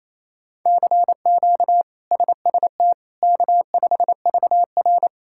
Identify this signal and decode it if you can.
Morse code, heard as a keyed tone, about 28 wpm, 700 Hz. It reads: CQ SST K5VR
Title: K1USN SST QSO Practice - 28wpm